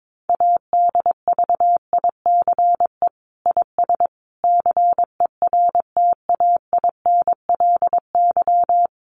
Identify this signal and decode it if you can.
Morse code, heard as a keyed tone, about 22 wpm, 700 Hz. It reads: ADVICE IS CERTAINLY
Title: Sentences from Top 1100 Words 22wpm